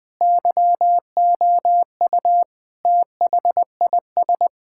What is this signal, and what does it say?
Morse code, heard as a keyed tone, about 20 wpm, 700 Hz. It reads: YOU THIS
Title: Sentences from Top 1000 Words - Review Part 6 - 20wpm